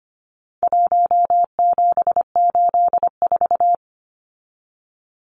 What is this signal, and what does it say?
Morse code, heard as a keyed tone, about 25 wpm, 700 Hz. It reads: 1784
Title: Numbers - 4-Digit 25wpm